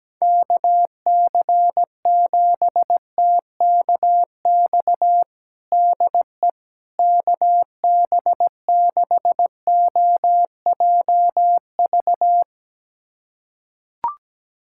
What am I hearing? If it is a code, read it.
Morse code, heard as a keyed tone, about 17 wpm, 700 Hz. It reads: KC7TKX DE KB6OJV
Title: USA Callsign DE Callsign - 6 Characters - 17wpm